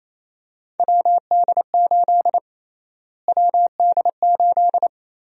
Morse code, heard as a keyed tone, about 27 wpm, 700 Hz. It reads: WD8 WD8
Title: USA Call Sign Prefixes - Repeated 2x - 28wpm